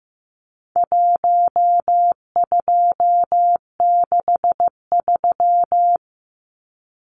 Morse code, heard as a keyed tone, about 15 wpm, 700 Hz. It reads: 1263
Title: Numbers - 4-Digit 15wpm